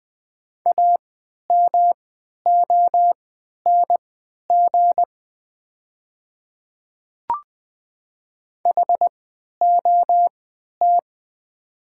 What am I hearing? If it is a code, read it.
Morse code, heard as a keyed tone, about 20 wpm, spaced as for 11 wpm, 690 Hz. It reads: AMONG HOT
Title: 3x Character Spacing - Top 500 Words - 20wpm